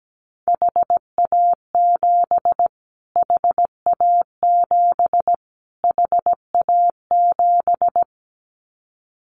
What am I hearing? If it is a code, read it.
Morse code, heard as a keyed tone, about 17 wpm, 700 Hz. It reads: HA7 HA7 HA7
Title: Callsign Prefixes for Top 25 DXCC - Repeated 3x - 17wpm